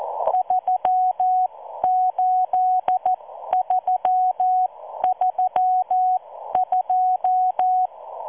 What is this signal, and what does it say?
Morse code, heard as a keyed tone, about 14 wpm, spaced as for 12 wpm, 740 Hz. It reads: 38332